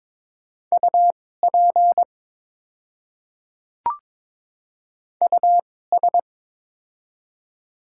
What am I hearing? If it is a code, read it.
Morse code, heard as a keyed tone, about 22 wpm, spaced as for 16 wpm, 690 Hz. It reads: UP US
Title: 2x Character Spacing - Top 200 Words - 22wpm